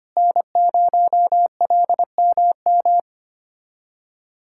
Morse code, heard as a keyed tone, about 25 wpm, 700 Hz. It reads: N0LMM
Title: USA Callsign DE Callsign 25wpm